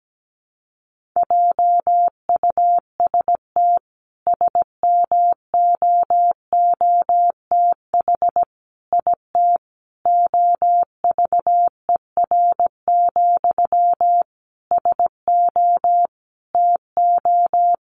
Morse code, heard as a keyed tone, about 17 wpm, 700 Hz. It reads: JUST SMOOTH IT OVER, SO TO
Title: Sentences from Top 1400 Words 17wpm